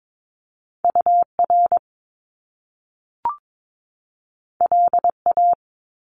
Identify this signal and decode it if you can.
Morse code, heard as a keyed tone, about 22 wpm, 700 Hz. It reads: UR LA
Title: Top Two Letter Combinations in Words 22wpm